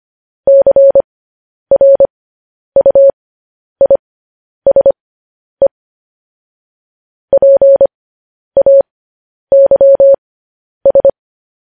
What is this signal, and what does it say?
Morse code, heard as a keyed tone, about 25 wpm, spaced as for 10 wpm, 550 Hz. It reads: CRUISE PAYS
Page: news headlines in morse code at 10 wpm